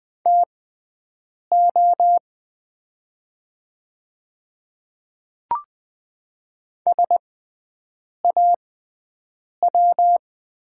Morse code, heard as a keyed tone, about 20 wpm, spaced as for 7 wpm, 700 Hz. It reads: TO SAW
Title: 6x Character Spacing - Top 500 Words - 20wpm